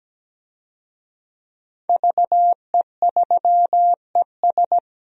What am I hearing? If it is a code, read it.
Morse code, heard as a keyed tone, about 17 wpm, 700 Hz. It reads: VE3ES
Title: DXCC-100 Call Signs - 5 Characters - 17wpm